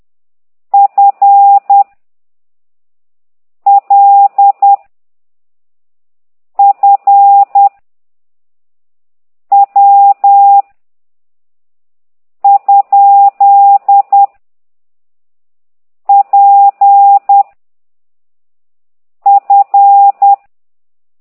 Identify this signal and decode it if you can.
Morse code, heard as a keyed tone, about 10 wpm, 800 Hz. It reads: F L F W ? P F
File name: ct2idw / exerc8